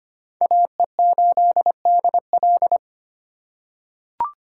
Morse code, heard as a keyed tone, about 25 wpm, 700 Hz. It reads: AE8DL
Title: USA Call Signs - 5 Characters - 2x Character Spacing - 25wpm